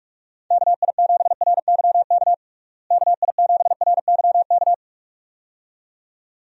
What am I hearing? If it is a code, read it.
Morse code, heard as a keyed tone, about 45 wpm, 700 Hz. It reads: KI7RYK KI7RYK
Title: USA Call Signs - 6 Characters - Repeated 2x - 45wpm